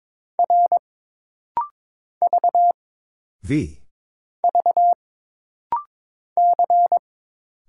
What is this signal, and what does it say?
Morse code, heard as a keyed tone, about 22 wpm, 700 Hz. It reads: R V V C
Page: This rapid-fire timing is geared to push your ICR (Instant Character Recognition) skills.